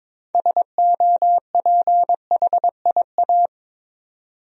Morse code, heard as a keyed tone, about 22 wpm, 700 Hz. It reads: SOPHIA
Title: USA Common Names - 22wpm